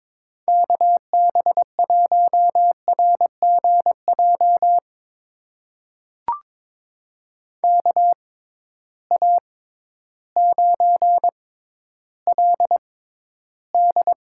Morse code, heard as a keyed tone, about 22 wpm, spaced as for 12 wpm, 700 Hz. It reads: KB1RGJ KA9LD
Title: USA Call Signs - 6 Characters - 6x Character Spacing - 22wpm